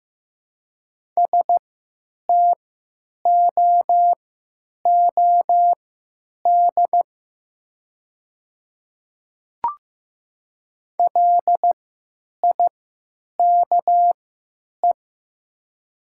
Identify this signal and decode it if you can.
Morse code, heard as a keyed tone, about 15 wpm, spaced as for 9 wpm, 700 Hz. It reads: STOOD LIKE
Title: 3x Character Spacing - Top 500 Words - 15wpm